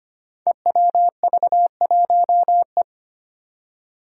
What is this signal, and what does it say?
Morse code, heard as a keyed tone, about 25 wpm, 700 Hz. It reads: EWV1E
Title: Five Challenging Letter-Number Combinations 25wpm